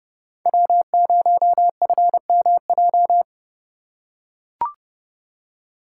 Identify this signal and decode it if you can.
Morse code, heard as a keyed tone, about 29 wpm, 700 Hz. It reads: W0FMJ